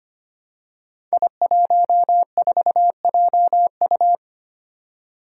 Morse code, heard as a keyed tone, about 25 wpm, 700 Hz. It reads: I14JU
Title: Five Challenging Letter-Number Combinations 25wpm